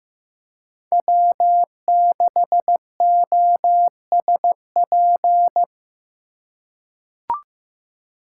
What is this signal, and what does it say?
Morse code, heard as a keyed tone, about 16 wpm, 700 Hz. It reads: W6OSP
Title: K1USN SST QSO Practice - 15wpm